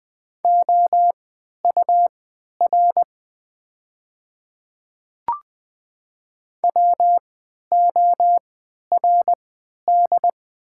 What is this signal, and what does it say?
Morse code, heard as a keyed tone, about 20 wpm, spaced as for 11 wpm, 690 Hz. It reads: OUR WORD
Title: 3x Character Spacing - Top 200 Words - 20wpm